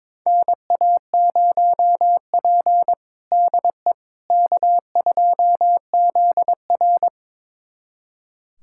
Morse code, heard as a keyed tone, about 21 wpm, 700 Hz. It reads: NA0P DE K2ZR
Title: USA Callsign DE Callsign - 4 Characters - 22wpm